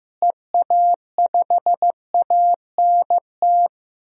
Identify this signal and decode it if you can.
Morse code, heard as a keyed tone, about 15 wpm, 700 Hz. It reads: EA5ANT
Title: DXCC-100 Call Signs - 6 Characters - Repeated 2x - 15wpm